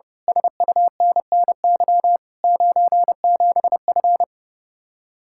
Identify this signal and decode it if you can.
Morse code, heard as a keyed tone, about 30 wpm, 700 Hz. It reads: SUNNY 97F
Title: Speed-Racing Common QSO Phrases - 30wpm